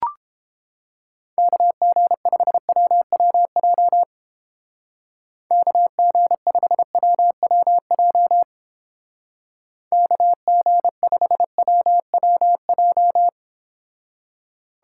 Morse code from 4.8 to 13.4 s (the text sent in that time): KG5WWJ KG5WWJ